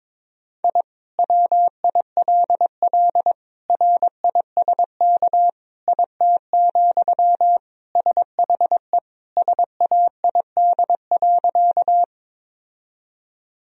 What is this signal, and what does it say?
Morse code, heard as a keyed tone, about 22 wpm, 700 Hz. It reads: I WILL RISK IT, SHE SAID.
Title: Sentences from Top 1000 Words - Review Part 4 - 22wpm